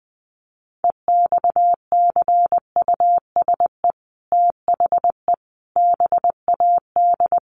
Morse code, heard as a keyed tone, about 20 wpm, 700 Hz. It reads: EXCUSE THE BAD
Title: Sentences from Top 800 Words 20wpm